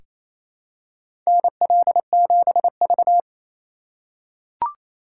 Morse code, heard as a keyed tone, about 28 wpm, 700 Hz. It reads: NL7V
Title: K1USN SST QSO Practice - 28wpm